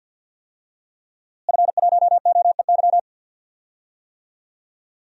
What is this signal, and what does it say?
Morse code, heard as a keyed tone, about 52 wpm, 700 Hz. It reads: F1OEY
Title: DXCC-100 Call Signs - 5 Characters - Repeated 3x - 50wpm